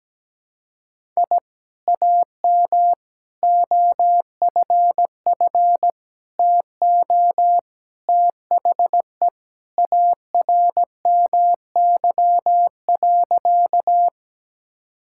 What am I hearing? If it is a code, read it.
Morse code, heard as a keyed tone, about 17 wpm, 700 Hz. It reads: I AM OFF TO THE ARMY.
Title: Sentences from Top 1000 Words - Review Part 5 - 17wpm